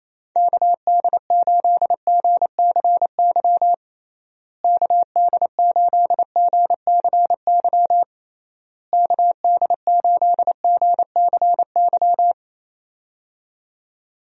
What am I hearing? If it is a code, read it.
Morse code, heard as a keyed tone, about 27 wpm, 700 Hz. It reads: KD8GCY KD8GCY KD8GCY
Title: USA Call Signs - 6 Characters - Repeated 3x - 28wpm